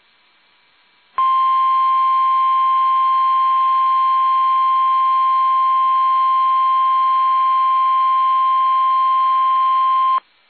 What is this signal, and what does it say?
Emergency Alert System (SAME) tones (the 1050 Hz attention tone). It is heard from 1.1 to 10.2 s.
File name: eas / KUSO / KUSO